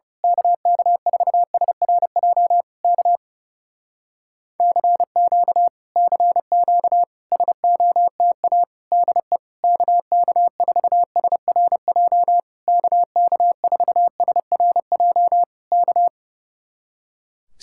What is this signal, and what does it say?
Morse code, heard as a keyed tone, about 28 wpm, 700 Hz. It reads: KK4SRJ K CQ CQ SOTA DE KK4SRJ KK4SRJ K
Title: Speed-Racing Common QSO Phrases Encore - 30wpm